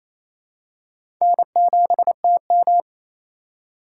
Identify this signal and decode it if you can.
Morse code, heard as a keyed tone, about 28 wpm, 700 Hz. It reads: N7TM
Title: USA Call Signs - 4 Characters - 3x Character Spacing - 28wpm